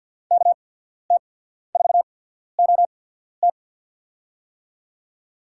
Morse code, heard as a keyed tone, about 53 wpm, spaced as for 14 wpm, 700 Hz. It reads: KT4OT
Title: USA Call Signs - 5 Characters - 8x Character Spacing - 50wpm